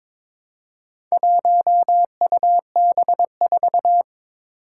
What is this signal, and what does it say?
Morse code, heard as a keyed tone, about 23 wpm, 700 Hz. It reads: 1UB4
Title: Four Challenging Letter-Number Combinations 22wpm